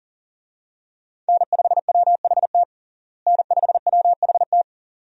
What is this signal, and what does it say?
Morse code, heard as a keyed tone, about 40 wpm, 700 Hz. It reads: N5WHT N5WHT
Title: USA Call Signs - 5 Characters - Repeated 2x - 40wpm